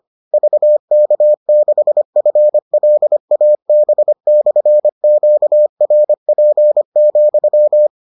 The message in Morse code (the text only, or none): VK6FLAB/QRP,